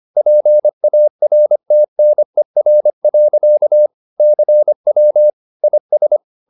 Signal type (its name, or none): Morse code